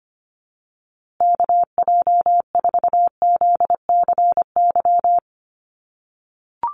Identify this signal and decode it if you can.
Morse code, heard as a keyed tone, about 25 wpm, 700 Hz. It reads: KJ4ZCY
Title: USA Call Signs Repeated 2x 25wpm